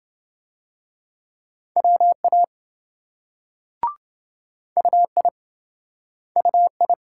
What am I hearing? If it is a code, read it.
Morse code, heard as a keyed tone, about 26 wpm, 700 Hz. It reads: WA UI UI